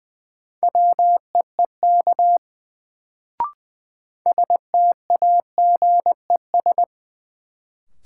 Morse code, heard as a keyed tone, about 20 wpm, 700 Hz. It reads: WEEK STAGES
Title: Sets of 5 Words Encore - New Word Drill - 20wpm